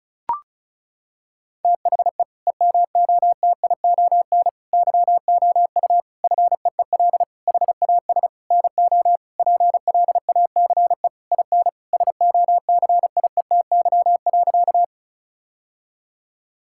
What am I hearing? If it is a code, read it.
Morse code, heard as a keyed tone, about 34 wpm, 700 Hz. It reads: THE EMOTION YOU FEEL HAS NO PLACE IN SOCIETY.
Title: Sentences from Top 1600 Words 35wpm